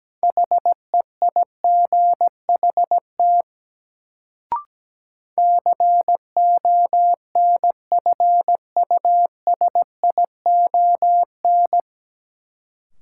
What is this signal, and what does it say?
Morse code, heard as a keyed tone, about 17 wpm, 700 Hz. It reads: HEIGHT CONFUSION
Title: Sentences from Top 1600 Words - New Word Drill - 17wpm